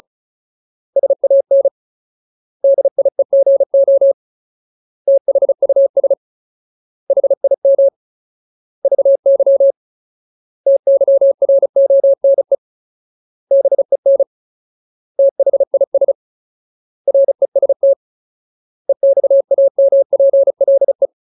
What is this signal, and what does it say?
Morse code, heard as a keyed tone, about 34 wpm, 550 Hz. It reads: SAN DIEGO THUS HIM VY TYRONE BEN THIS REST EXAMPLE
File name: Ultimate Word List / 35WPM Ultimate